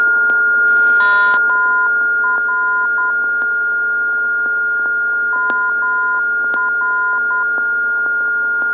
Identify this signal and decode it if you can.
Morse code, heard as a keyed tone, about 10 wpm, 1020 Hz, with noise. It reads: MR MR
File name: ndb zvok / MR-334khz